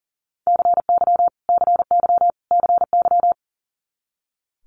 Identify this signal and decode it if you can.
Morse code, heard as a keyed tone, about 41 wpm, 700 Hz. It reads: CY CY CY
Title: Multi-letter Phonemes Repeated 3x 40wpm